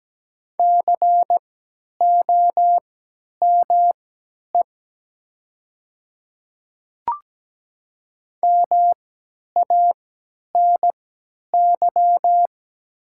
Morse code, heard as a keyed tone, about 17 wpm, spaced as for 10 wpm, 700 Hz. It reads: COME MANY
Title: 3x Character Spacing - Top 100 Words - 17wpm